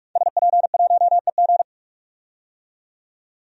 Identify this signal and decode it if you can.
Morse code, heard as a keyed tone, about 44 wpm, 690 Hz. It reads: SP1EG
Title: DXCC-100 Call Signs - 5 Characters - 45wpm